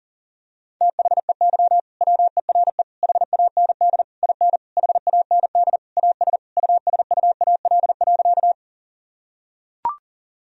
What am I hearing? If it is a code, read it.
Morse code, heard as a keyed tone, about 40 wpm, 700 Hz. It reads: THEY WERE HAND IN HAND AS USUAL.
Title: Sentences from Top 1000 Words - Review Part 2 - 40wpm